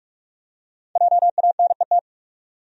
Morse code, heard as a keyed tone, about 46 wpm, 700 Hz. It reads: JANET
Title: USA Common Names - Max 5 Letters - 45wpm